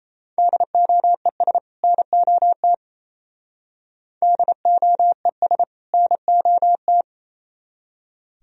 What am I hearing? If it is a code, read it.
Morse code, heard as a keyed tone, about 34 wpm, 700 Hz. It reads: DOES NOT DOES NOT
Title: Speed-Racing - Sets of 2 Words 28wpm